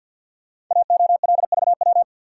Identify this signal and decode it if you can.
Morse code, heard as a keyed tone, about 53 wpm, 700 Hz. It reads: AOLVW
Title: Code Groups - 5 Letters - 50wpm